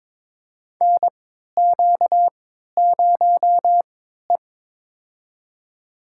Morse code, heard as a keyed tone, about 22 wpm, spaced as for 13 wpm, 700 Hz. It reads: NQ0E
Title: USA Call Signs - 4 Characters - 3x Character Spacing - 22wpm